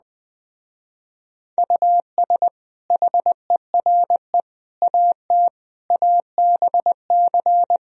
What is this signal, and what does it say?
Morse code, heard as a keyed tone, about 21 wpm, 700 Hz. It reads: US HERE AT ABC